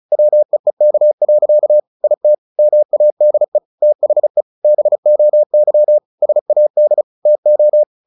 Morse code, heard as a keyed tone, about 34 wpm, 600 Hz. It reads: WEEK. IT MADE THE BOY SAD TO
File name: the old man and the sea / the old man and the sea 1 35wpm